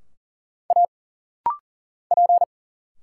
Morse code, heard as a keyed tone, about 40 wpm, 700 Hz. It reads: A P
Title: Single Letters - Rapid-Fire 40wpm